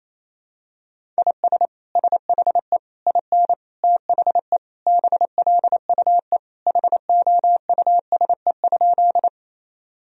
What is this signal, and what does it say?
Morse code, heard as a keyed tone, about 28 wpm, 700 Hz. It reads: IS SHE IN THE BLUE HOUSE?
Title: Sentences from Top 1000 Words - Review Part 3 - 28wpm